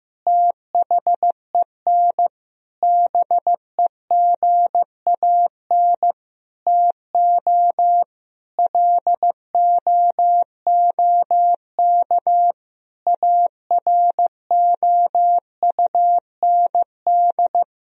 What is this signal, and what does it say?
Morse code, heard as a keyed tone, about 15 wpm, 700 Hz. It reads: THEN BEGAN TO LOOK AROUND
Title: Sentences from Top 1000 Words - Review Part 3 - 15wpm